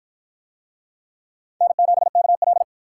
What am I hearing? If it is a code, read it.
Morse code, heard as a keyed tone, about 51 wpm, 700 Hz. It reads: N7KL